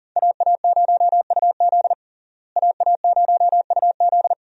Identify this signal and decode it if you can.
Morse code, heard as a keyed tone, about 40 wpm, 700 Hz. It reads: AA0UZ AA0UZ
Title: USA Call Signs - 5 Characters - Repeated 2x - 40wpm